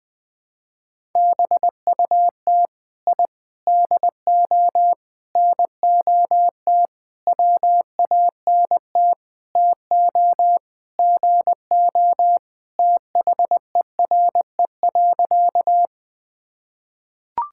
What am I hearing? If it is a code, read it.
Morse code, heard as a keyed tone, about 20 wpm, 700 Hz. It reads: BUT I DO NOT WANT TO GO THERE.
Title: Sentences with 5-Word Phrases - 20wpm